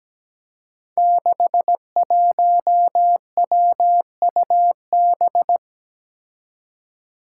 Morse code, heard as a keyed tone, about 17 wpm, 700 Hz. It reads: 61WUB
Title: Five Challenging Letter-Number Combinations 17wpm